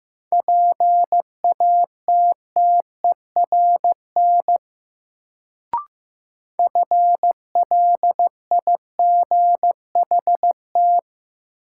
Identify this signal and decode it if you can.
Morse code, heard as a keyed tone, about 15 wpm, 700 Hz. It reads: PATTERN FLIGHT
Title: Sentences from Top 1000 Words - New Word Drill - 15wpm